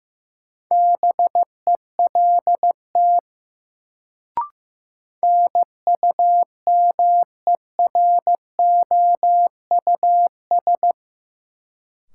Morse code, heard as a keyed tone, about 15 wpm, 700 Hz. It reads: BELT NUMEROUS
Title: Sentences from Top 1400 Words - New Word Drill Alt - 15wpm